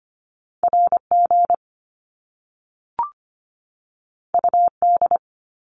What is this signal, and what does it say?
Morse code, heard as a keyed tone, about 25 wpm, 700 Hz. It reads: RG UD
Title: Top Two Letter Combinations in Words 25wpm